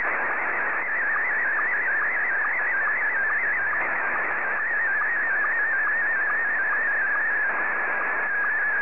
SSTV (slow-scan television), Scottie 1.